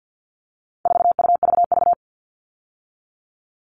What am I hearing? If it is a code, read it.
Morse code, heard as a keyed tone, about 50 wpm, 700 Hz. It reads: 4UVV
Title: All Cut Numbers - 4-Digit 50wpm